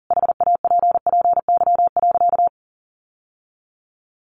Morse code, heard as a keyed tone, about 40 wpm, 700 Hz. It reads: HAPPY.
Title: Sentences from Top 1000 Words 40wpm